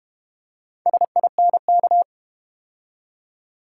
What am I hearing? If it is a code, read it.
Morse code, heard as a keyed tone, about 32 wpm, 700 Hz. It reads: SINK